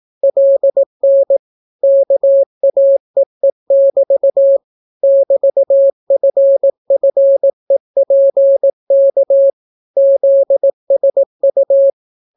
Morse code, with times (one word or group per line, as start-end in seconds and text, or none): 0.2-1.4 LN
1.8-4.6 KAEE<BT>
5.0-9.5 <BT>FFEPK
10.0-11.9 ZSU